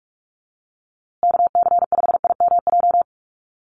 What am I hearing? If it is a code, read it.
Morse code, heard as a keyed tone, about 44 wpm, 690 Hz. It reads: KC5IMJ
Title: USA Call Signs Repeated 3x 45wpm